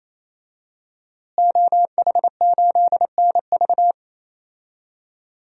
Morse code, heard as a keyed tone, about 28 wpm, 700 Hz. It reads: OH8NV
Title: DXCC-100 Call Signs - 5 Characters - Repeated 3x - 28wpm